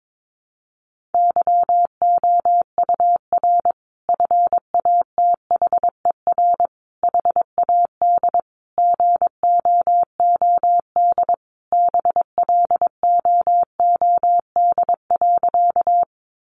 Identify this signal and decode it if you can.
Morse code, heard as a keyed tone, about 22 wpm, 700 Hz. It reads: YOUR FATHER HAD GOOD BLOOD.
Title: Sentences from Top 1100 Words 22wpm